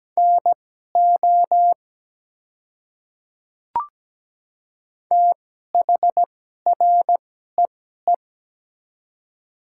Morse code, heard as a keyed tone, about 17 wpm, spaced as for 12 wpm, 700 Hz. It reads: NO THREE
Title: 2x Character Spacing - Top 200 Words - 17wpm